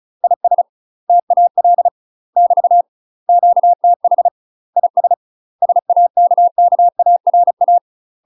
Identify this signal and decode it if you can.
Morse code, heard as a keyed tone, about 34 wpm, 700 Hz. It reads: IS TAL <BT> QTH IS SAKKARA
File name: UK MASTER QSOs TRAINING / 35WPM / UK MASTER QSOs 0040